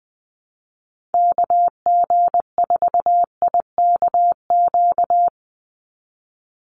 Morse code, heard as a keyed tone, about 20 wpm, 700 Hz. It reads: KG4IKQ
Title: Call Signs Repeated 2x - 20wpm